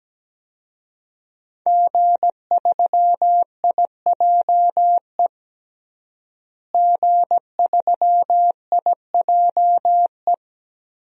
Morse code, heard as a keyed tone, about 17 wpm, 700 Hz. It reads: G3IJE G3IJE
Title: DXCC-100 Call Signs - 5 Characters - Repeated 2x - 17wpm